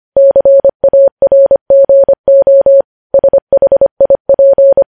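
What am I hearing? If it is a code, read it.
Morse code, heard as a keyed tone, about 25 wpm, 550 Hz. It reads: CARGO SHIP
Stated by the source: Morse code news headlines at 25 wpm.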